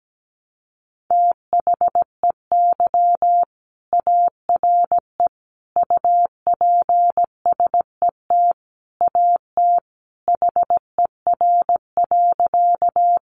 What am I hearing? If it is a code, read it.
Morse code, heard as a keyed tone, about 17 wpm, 700 Hz. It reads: THEY ARE UPSET AT HER.
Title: Sentences from Top 800 Words 17wpm